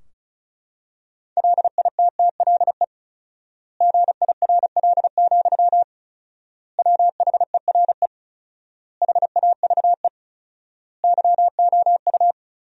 Morse code, heard as a keyed tone, about 36 wpm, 700 Hz. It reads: LITTLE GIRL, WHERE HAVE YOU
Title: 4x Word-Spacing - Sentences from Top 400 Words - 35wpm